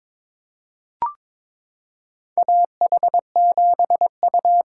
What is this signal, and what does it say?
Morse code, heard as a keyed tone, about 22 wpm, 700 Hz. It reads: AH7U
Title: USA Callsign DE Callsign - 4 Characters - 22wpm